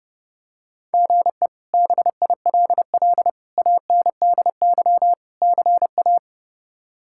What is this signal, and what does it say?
Morse code, heard as a keyed tone, about 32 wpm, 700 Hz. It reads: GE BILL ANDY CA
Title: K1USN SST QSO Practice - 30wpm